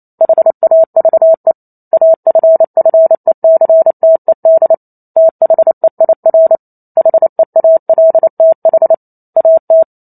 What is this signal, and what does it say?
Morse code, heard as a keyed tone, about 28 wpm, 650 Hz. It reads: HAVE AFFECTED THEIR HEALTH AT